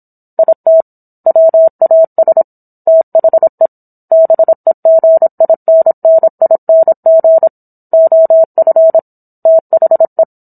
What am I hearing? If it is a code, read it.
Morse code, heard as a keyed tone, about 27 wpm, 650 Hz. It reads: IT WAS THE BEGINNING OF THE
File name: Book One / 17-Chapter 17 - 26 WPM